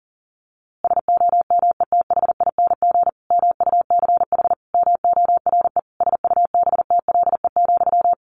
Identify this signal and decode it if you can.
Morse code, heard as a keyed tone, about 37 wpm, 700 Hz. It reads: SOMETHING MUCH MORE SUBTLE,
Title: A Sound Of Thunder 40wpm